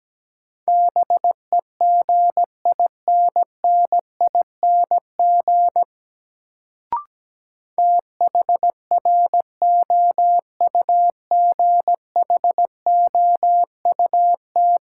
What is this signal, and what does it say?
Morse code, heard as a keyed tone, about 17 wpm, 700 Hz. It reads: BEGINNING THROUGHOUT
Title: Sentences from Top 800 Words - New Word Drill - 17wpm